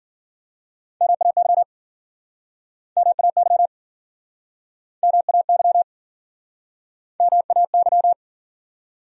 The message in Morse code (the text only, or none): MAY MAY MAY MAY